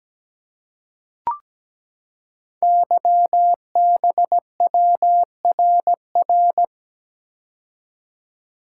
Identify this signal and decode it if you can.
Morse code, heard as a keyed tone, about 17 wpm, 700 Hz. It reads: YBWRR
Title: Code Groups - 5 Letters - 17wpm